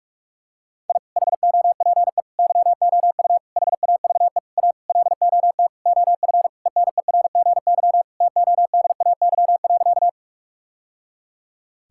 Morse code, heard as a keyed tone, about 41 wpm, 700 Hz. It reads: I HOPE YOU HAVE A LOT OF ENERGY TODAY.
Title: Sentences from Top 1000 Words - Review Part 6 - 45wpm